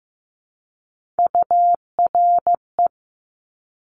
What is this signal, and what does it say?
Morse code, heard as a keyed tone, about 15 wpm, 700 Hz. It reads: URE
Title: Top Three Letter Combinations in Words 15wpm